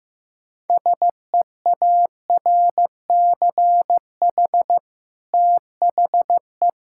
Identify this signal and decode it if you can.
Morse code, heard as a keyed tone, about 14 wpm, 700 Hz. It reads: SEARCH THE
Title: Sentences from Top 1000 Words - Review Part 6 - 15wpm